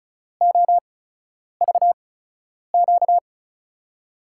Morse code, heard as a keyed tone, about 36 wpm, spaced as for 10 wpm, 700 Hz. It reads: OVQ
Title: USA Call Signs - 6 Characters - 8x Character Spacing - 35wpm